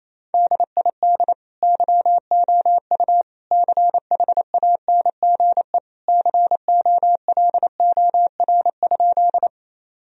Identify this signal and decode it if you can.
Morse code, heard as a keyed tone, about 28 wpm, 700 Hz. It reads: DID YOU CHANGE COLOR?
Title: Sentences from Top 1000 Words - Review Part 2 - 28wpm